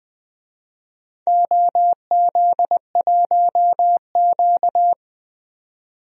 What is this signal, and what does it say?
Morse code, heard as a keyed tone, about 21 wpm, 700 Hz. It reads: OZ1Q
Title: DXCC-100 Call Signs - 4 Characters - 20wpm